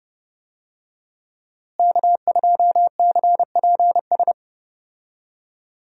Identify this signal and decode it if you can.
Morse code, heard as a keyed tone, about 30 wpm, 700 Hz. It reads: K2CPS
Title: USA Call Signs - 5 Characters - Repeated 3x - 30wpm